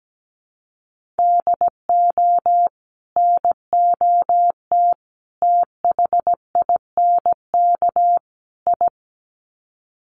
Morse code, heard as a keyed tone, about 17 wpm, 700 Hz. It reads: DO NOT THINK I
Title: Sets of 4 Words 17wpm